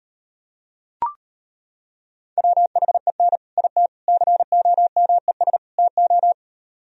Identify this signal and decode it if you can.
Morse code, heard as a keyed tone, about 31 wpm, 700 Hz. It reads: WHEN IT COMES TO
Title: Speed-Racing - Sets of 4 Words 25wpm